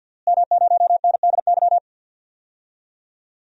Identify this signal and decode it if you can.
Morse code, heard as a keyed tone, about 47 wpm, 700 Hz. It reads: M0NDY
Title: DXCC-100 Call Signs - 5 Characters - Repeated 3x - 50wpm